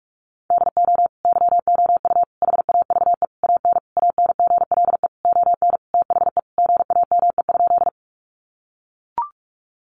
Morse code, heard as a keyed tone, about 44 wpm, 700 Hz. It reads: DO YOU HAVE AN ANGLE ON THE GAME?
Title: Sentences from Top 1500 Words 45wpm